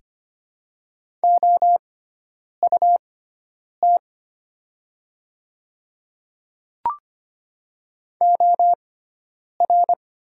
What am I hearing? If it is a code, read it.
Morse code, heard as a keyed tone, about 25 wpm, spaced as for 9 wpm, 700 Hz. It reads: OUT OR
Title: 6x Character Spacing - Top 100 Words - 25wpm